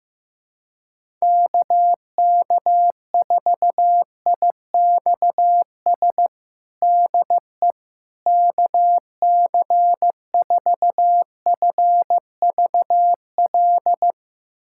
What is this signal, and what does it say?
Morse code, heard as a keyed tone, about 16 wpm, 700 Hz. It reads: KK4IXS DE KC4FVL
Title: USA Callsign DE Callsign - 6 Characters - 15wpm